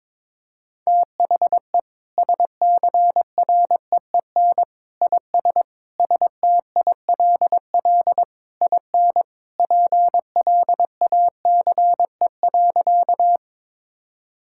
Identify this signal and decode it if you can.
Morse code, heard as a keyed tone, about 22 wpm, 700 Hz. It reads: THE SCREEN IS STILL IN PLACE.
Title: Sentences from Top 1000 Words - Review Part 6 - 22wpm